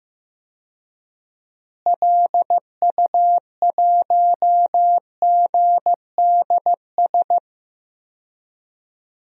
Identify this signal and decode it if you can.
Morse code, heard as a keyed tone, about 15 wpm, 700 Hz. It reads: LU1GDS
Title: DXCC-100 Call Signs - 6 Characters - Repeated 3x - 15wpm